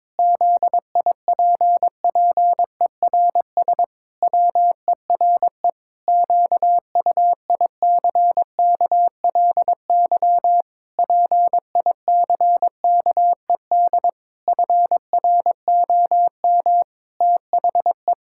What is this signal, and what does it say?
Morse code, heard as a keyed tone, about 22 wpm, 700 Hz. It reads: ZIPPERS WERE QUICKLY PICKED FROM THE
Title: Pangram Challenge v2 - 22wpm